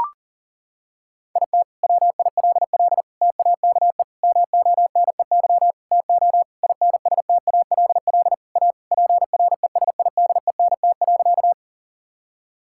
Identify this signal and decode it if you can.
Morse code, heard as a keyed tone, about 40 wpm, 700 Hz. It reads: IT WILL TAKE MONEY TO INSTALL A PRESIDENT.
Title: Sentences from Top 1600 Words 40wpm